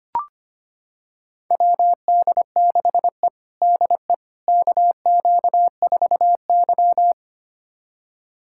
Morse code, heard as a keyed tone, about 25 wpm, 700 Hz. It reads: WD6E DE KQ4Y
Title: USA Callsign DE Callsign - 4 Characters - 25wpm